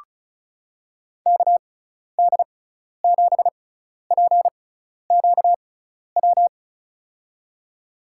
Morse code, heard as a keyed tone, about 34 wpm, spaced as for 12 wpm, 700 Hz. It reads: KD7PQW
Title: USA Call Signs - 6x Character Spacing - 35wpm